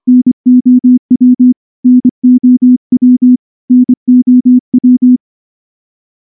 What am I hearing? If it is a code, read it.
Morse code, heard as a keyed tone, about 23 wpm, 250 Hz. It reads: NOW NOW NOW
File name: morse-training-25WPM